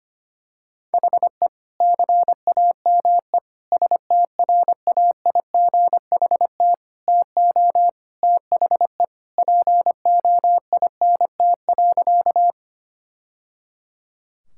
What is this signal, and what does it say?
Morse code, heard as a keyed tone, about 25 wpm, 700 Hz. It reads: HE CAME STRAIGHT TO THE POINT.
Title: Sentences from Top 1000 Words - Review Part 6 - 25wpm